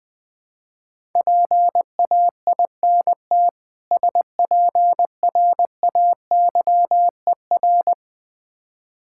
Morse code, heard as a keyed tone, about 20 wpm, 700 Hz. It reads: PAINT SPRAYER
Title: Difficult Words with Hints - 20wpm